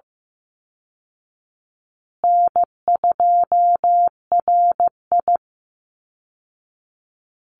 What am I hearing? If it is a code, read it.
Morse code, heard as a keyed tone, about 15 wpm, 700 Hz. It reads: N2RI
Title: Easy Call Signs Repeated 3x - 15wpm